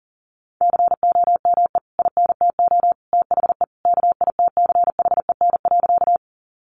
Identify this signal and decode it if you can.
Morse code, heard as a keyed tone, about 38 wpm, 700 Hz. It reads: COME INTO THE KITCHEN.
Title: Sentences from Top 1400 Words 40wpm